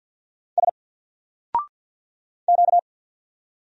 Morse code, heard as a keyed tone, about 50 wpm, 700 Hz. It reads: S Q
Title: Single Letter-Number 50wpm